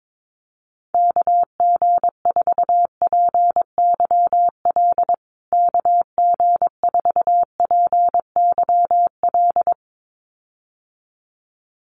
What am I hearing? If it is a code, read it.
Morse code, heard as a keyed tone, about 22 wpm, 700 Hz. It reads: KG4PYL KG4PYL
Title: USA Call Signs Repeated 2x 22wpm